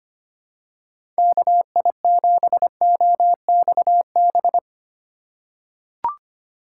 Morse code, heard as a keyed tone, about 25 wpm, 700 Hz. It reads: KI7OXB
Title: USA Call Signs - 6 Characters - 3x Character Spacing - 25wpm